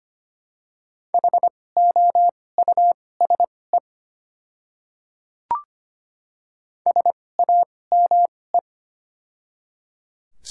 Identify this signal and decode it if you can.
Morse code, heard as a keyed tone, about 25 wpm, spaced as for 18 wpm, 700 Hz. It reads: HOUSE SAME
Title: 2x Character Spacing - Top 200 Words - 25wpm